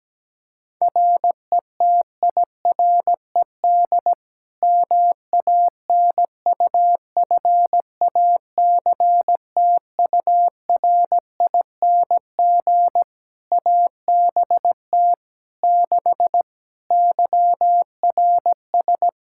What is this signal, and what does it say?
Morse code, heard as a keyed tone, about 17 wpm, 700 Hz. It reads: RETIRED MANUFACTURING ABT 6 YRS
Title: Practice QSO - Exchange 3 - More Info - 17wpm